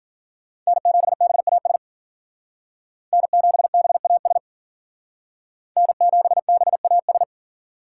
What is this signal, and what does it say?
Morse code, heard as a keyed tone, about 50 wpm, 700 Hz. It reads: N7BAS N7BAS N7BAS
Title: Speed Racing - USA 1x3 Call Signs 40wpm